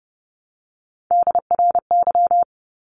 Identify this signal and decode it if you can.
Morse code, heard as a keyed tone, about 30 wpm, 700 Hz. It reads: DRY
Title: Top 500 Words Repeated 30wpm